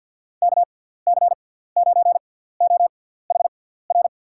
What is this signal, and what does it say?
Morse code, heard as a keyed tone, about 48 wpm, spaced as for 17 wpm, 700 Hz. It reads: KC9OHR
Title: USA Call Signs - 6 Characters - 6x Character Spacing - 50wpm